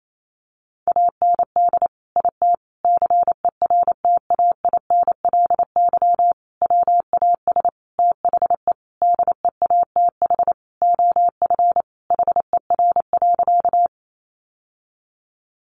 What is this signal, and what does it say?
Morse code, heard as a keyed tone, about 28 wpm, 700 Hz. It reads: AND IT CERTAINLY WAS THE DEATH OF HER.
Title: Sentences from Top 1100 Words 28wpm